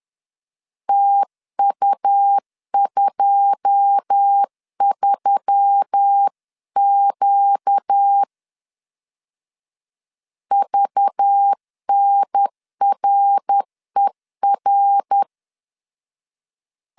Morse code, heard as a keyed tone, about 10 wpm, 800 Hz. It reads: TU23Q VNRER